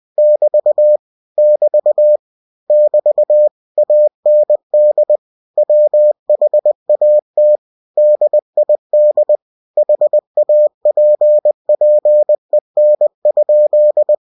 Morse code, heard as a keyed tone, about 20 wpm, 600 Hz. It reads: <BT> <BT> <BT> AND WHAT DID HAPPEN?
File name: cw-books / winnie-the-pooh / winnie-20wpm-page-0110